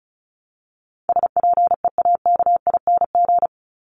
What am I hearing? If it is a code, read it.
Morse code, heard as a keyed tone, about 34 wpm, 700 Hz. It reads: SPEAKING
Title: ING Suffix - Top 100 Words with Base Words 35wpm